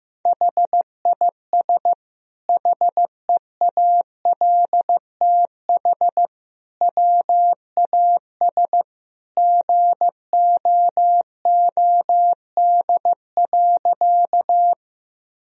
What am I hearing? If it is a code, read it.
Morse code, heard as a keyed tone, about 15 wpm, 700 Hz. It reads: HIS HEALTH WAS GOOD.
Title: Sentences from Top 1000 Words - Review Part 3 - 15wpm